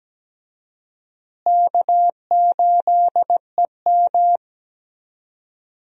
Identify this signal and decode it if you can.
Morse code, heard as a keyed tone, about 17 wpm, 700 Hz. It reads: K8EM
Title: USA Call Signs - 4 Characters - Repeated 3x - 17wpm